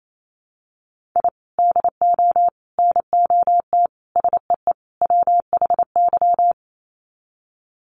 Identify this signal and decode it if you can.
Morse code, heard as a keyed tone, about 28 wpm, 700 Hz. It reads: I DO NOT SEE WHY
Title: Sets of 5 Words Encore 28wpm